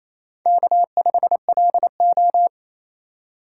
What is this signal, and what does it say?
Morse code, heard as a keyed tone, about 27 wpm, 700 Hz. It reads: K5LO
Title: USA Call Signs - 3x Character Spacing - 28wpm